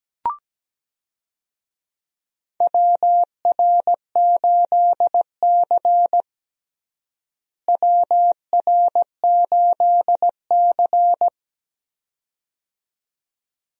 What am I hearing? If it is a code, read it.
Morse code, heard as a keyed tone, about 17 wpm, 700 Hz. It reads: WR8C WR8C
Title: USA Call Signs - 4 Characters - Repeated 2x - 17wpm